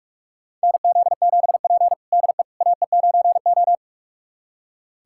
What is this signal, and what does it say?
Morse code, heard as a keyed tone, about 44 wpm, 700 Hz. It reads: NZ7P DE AE9O
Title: USA Callsign DE Callsign - 4 Characters - 45wpm